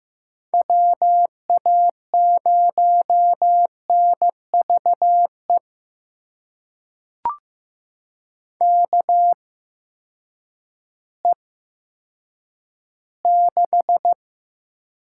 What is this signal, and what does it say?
Morse code, heard as a keyed tone, about 15 wpm, spaced as for 9 wpm, 700 Hz. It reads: WA0NVE KE6
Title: USA Call Signs - 6 Characters - 8x Character Spacing - 15wpm